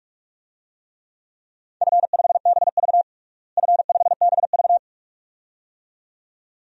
Morse code, heard as a keyed tone, about 46 wpm, 700 Hz. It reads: F5BV F5BV
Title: DXCC-100 Call Signs - 4 Characters - Repeated 2x - 45wpm